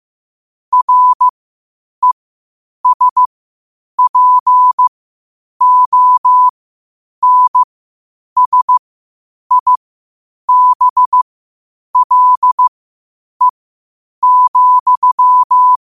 Morse code, heard as a keyed tone, about 15 wpm, spaced as for 8 wpm, 1000 Hz. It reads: RESPONSIBLE,
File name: qotdfiles / 8wpm